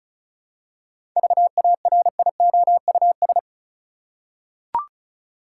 Morse code, heard as a keyed tone, about 36 wpm, 700 Hz. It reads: VARIOUS
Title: Seven Letter Words - 35wpm